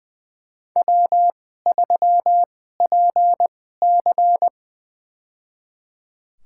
Morse code, heard as a keyed tone, about 20 wpm, spaced as for 15 wpm, 700 Hz. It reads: W3PC
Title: USA Call Signs - 2x Character Spacing - 20wpm